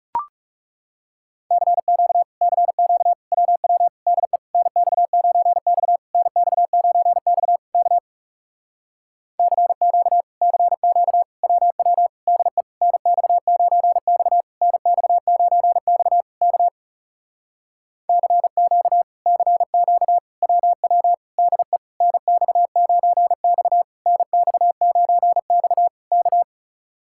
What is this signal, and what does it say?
Morse code, heard as a keyed tone, about 37 wpm, 700 Hz. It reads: CQ CQ WW DE NX9X NX9X K CQ CQ WW DE NX9X NX9X K CQ CQ WW DE NX9X NX9X K
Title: Speed-Racing Common QSO Phrases - 30wpm